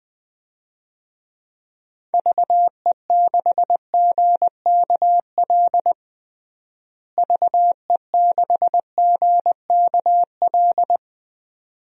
Morse code, heard as a keyed tone, about 21 wpm, 700 Hz. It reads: VE6GKL VE6GKL
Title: DXCC-100 Call Signs - 6 Characters - Repeated 2x - 20wpm